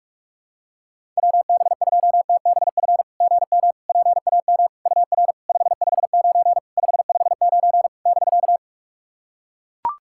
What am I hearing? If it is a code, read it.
Morse code, heard as a keyed tone, about 44 wpm, 690 Hz. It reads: WB2TBF GM PAM UR 559 559 <BK>
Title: SOTA QSO Practice 45wpm